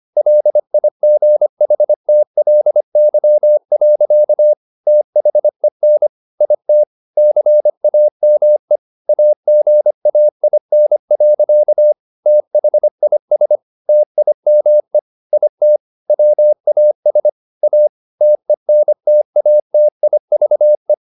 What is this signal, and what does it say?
Morse code, heard as a keyed tone, about 25 wpm, 600 Hz. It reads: LIGHTLY. THEN IT CAME AGAIN. THIS TIME IT WAS A TENTATIVE
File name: the old man and the sea part 1 25wpm